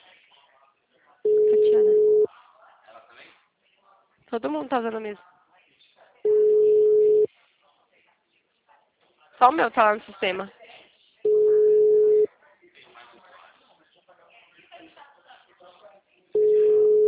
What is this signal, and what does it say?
Ringback tone (Europe, 425 Hz), with speech.
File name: out- -2008- - - .29694